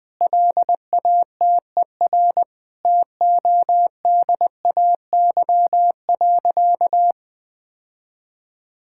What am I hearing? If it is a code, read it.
Morse code, heard as a keyed tone, about 19 wpm, 700 Hz. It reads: LATER TODAY.
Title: Sentences from Top 1000 Words - Review Part 6 - 20wpm